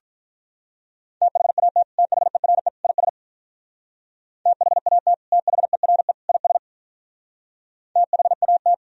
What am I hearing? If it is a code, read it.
Morse code, heard as a keyed tone, about 44 wpm, 700 Hz. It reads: THAT THERE IS THAT THERE IS THAT
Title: Speed-Racing - Sets of 3 Words 35wpm